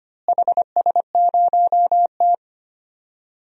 Morse code, heard as a keyed tone, about 25 wpm, 700 Hz. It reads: HS0T
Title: DXCC-100 Call Signs - 4 Characters - Repeated 3x - 25wpm